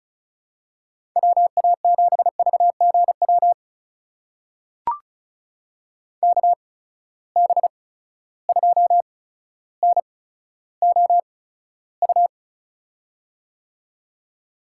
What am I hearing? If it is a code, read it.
Morse code, heard as a keyed tone, about 36 wpm, spaced as for 15 wpm, 700 Hz. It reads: WA7VGW KB2NOU
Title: USA Call Signs - 6 Characters - 8x Character Spacing - 35wpm